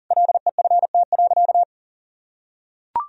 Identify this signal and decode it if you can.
Morse code, heard as a keyed tone, about 40 wpm, 700 Hz. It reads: LEFT.
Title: Sentences from Top 1000 Words - Review Part 2 - 40wpm